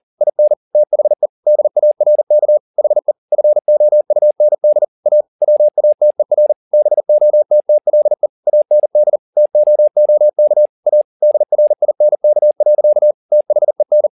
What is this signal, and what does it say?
Morse code, heard as a keyed tone, about 40 wpm, 600 Hz. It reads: IN THE DARK HE FOUND A WATER BOTTLE AND TOOK A DRINK. THEN
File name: the old man and the sea 2 40wpm